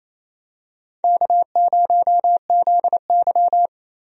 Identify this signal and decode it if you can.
Morse code, heard as a keyed tone, about 27 wpm, 700 Hz. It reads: K0ZY
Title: USA Call Signs - 4 Characters - Repeated 2x - 28wpm